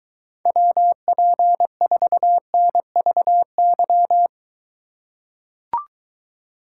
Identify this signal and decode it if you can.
Morse code, heard as a keyed tone, about 23 wpm, 700 Hz. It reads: WP4NVY